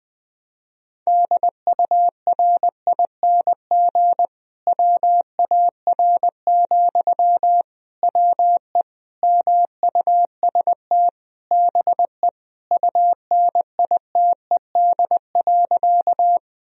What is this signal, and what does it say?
Morse code, heard as a keyed tone, about 21 wpm, 700 Hz. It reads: DURING WAR, WE MUST BE UNITED.
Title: Sentences from Top 1000 Words - Review Part 4 - 20wpm